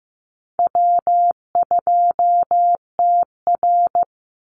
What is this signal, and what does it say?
Morse code, heard as a keyed tone, about 15 wpm, 700 Hz. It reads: W2TR
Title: Easy Call Signs Repeated 2x - 15wpm